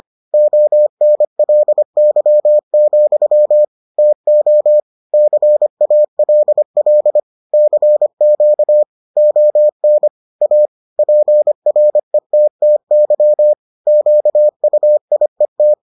Morse code, heard as a keyed tone, about 25 wpm, 600 Hz. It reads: ONLY, TO CALL CQ ON A PRETTY QUIET